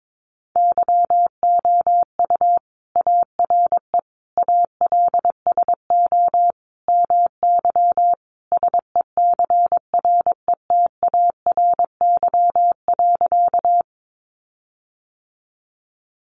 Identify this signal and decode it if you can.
Morse code, heard as a keyed tone, about 22 wpm, 700 Hz. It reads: YOU ARE ALSO MY SECRETARY.
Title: Sentences from Top 1300 Words 22wpm